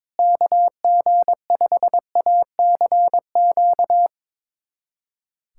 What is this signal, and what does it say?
Morse code, heard as a keyed tone, about 22 wpm, 700 Hz. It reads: KG5ACQ
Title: USA Callsign DE Callsign - 6 Characters - 22wpm